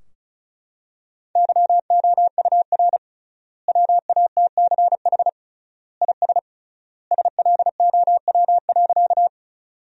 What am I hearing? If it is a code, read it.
Morse code, heard as a keyed tone, about 34 wpm, 700 Hz. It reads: YOUR WATCH IS SLOW.
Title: 3x Word-Spacing - Sentences from Top 400 Words - 35wpm